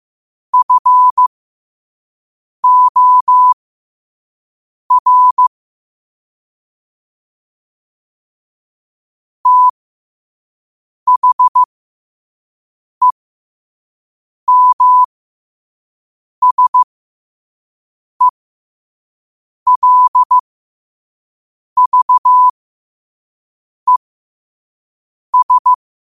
Morse code, heard as a keyed tone, about 15 wpm, spaced as for 5 wpm, 1000 Hz. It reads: FOR THEMSELVES
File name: qotdfiles / 5wpm